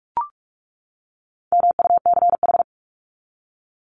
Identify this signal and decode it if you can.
Morse code, heard as a keyed tone, about 44 wpm, 700 Hz. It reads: MUCH
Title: Top 300 Words 45wpm